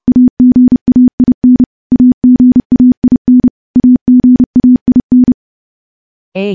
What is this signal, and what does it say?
Morse code, heard as a keyed tone, about 30 wpm, 250 Hz. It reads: AGAIN AGAIN AGAIN
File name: morse / morse-training-30WPM